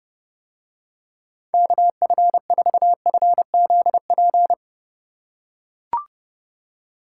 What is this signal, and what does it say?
Morse code, heard as a keyed tone, about 30 wpm, 700 Hz. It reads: KF4FZP